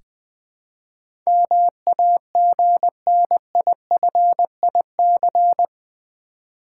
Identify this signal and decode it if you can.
Morse code, heard as a keyed tone, about 21 wpm, 700 Hz. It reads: MAGNIFIC
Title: Build A Word - 4 Letters At A Time - 11-Letter Words - 20wpm